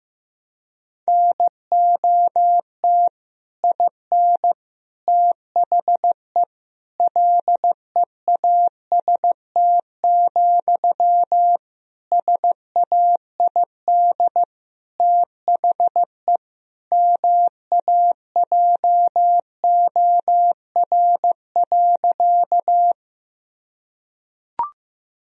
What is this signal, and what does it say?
Morse code, heard as a keyed tone, about 15 wpm, 700 Hz. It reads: NOT IN THE LEAST, SAID THE MAJOR.
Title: Sentences from Top 1000 Words - Review Part 4 - 15wpm